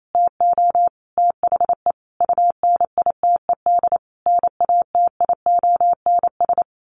Morse code, heard as a keyed tone, about 28 wpm, 700 Hz. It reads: TO THE UNITED NATIONS
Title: Sets of 5 Words Encore2 28wpm